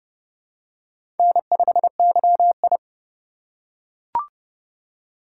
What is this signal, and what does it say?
Morse code, heard as a keyed tone, about 31 wpm, 700 Hz. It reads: N5YI
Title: USA Call Signs - 4 Characters - 8x Character Spacing - 30wpm